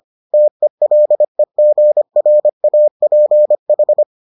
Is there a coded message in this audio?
Yes: Morse code from 0.3 to 4.0 s, heard as a keyed tone, about 25 wpm, 600 Hz. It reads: TELEGRAPH